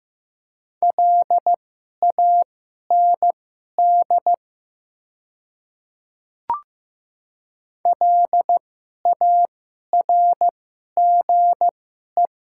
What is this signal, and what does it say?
Morse code, heard as a keyed tone, about 15 wpm, spaced as for 11 wpm, 690 Hz. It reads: LAND LARGE
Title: 2x Character Spacing - Top 200 Words - 15wpm